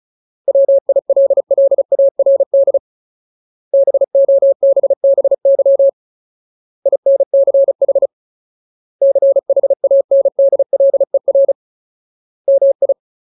Morse code, heard as a keyed tone, about 34 wpm, 550 Hz. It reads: WILLARD BOBBY INCH CHANDLER MI
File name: Ultimate Word List / 35WPM Ultimate